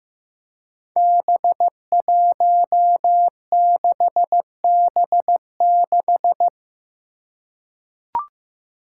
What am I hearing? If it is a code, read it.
Morse code, heard as a keyed tone, about 15 wpm, 700 Hz. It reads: B16B6
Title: Five Challenging Letter-Number Combinations 15wpm